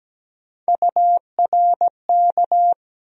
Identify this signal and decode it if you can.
Morse code, heard as a keyed tone, about 17 wpm, 700 Hz. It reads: URK
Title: USA Call Sign Postfixes - 3 Letters 17wpm